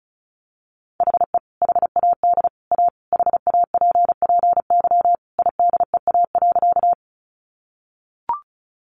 Morse code, heard as a keyed tone, about 34 wpm, 700 Hz. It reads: HE HAD A HAPPY IDEA.
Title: Sentences from Top 1000 Words 35wpm